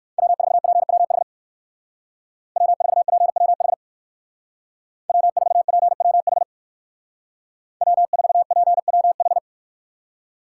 Morse code, heard as a keyed tone, about 63 wpm, 700 Hz. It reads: W4PWH W4PWH W4PWH W4PWH